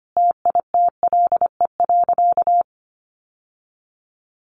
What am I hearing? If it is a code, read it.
Morse code, heard as a keyed tone, about 25 wpm, 700 Hz. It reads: TITLE.
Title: Sentences from Top 1200 Words 25wpm